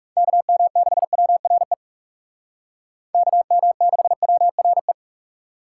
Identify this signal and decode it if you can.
Morse code, heard as a keyed tone, about 37 wpm, 700 Hz. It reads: KM6WRE KM6WRE
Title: Speed Racing - USA 2x3 Call Signs 30wpm